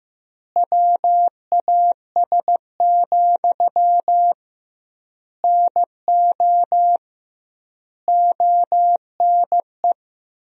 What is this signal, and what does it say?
Morse code, heard as a keyed tone, about 15 wpm, 700 Hz. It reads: WAS, NO ONE
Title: 2x Word-Spacing - Sentences from Top 400 Words - 15wpm